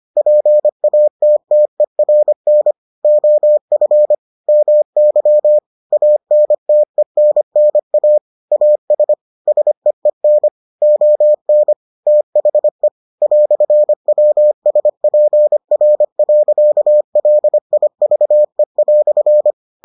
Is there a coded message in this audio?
Yes: Morse code from 0.2 to 19.5 s, heard as a keyed tone, about 25 wpm, 600 Hz. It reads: PATTERN OF MY ANTENNA AS SEEN ON THE "WSPR.LIVE"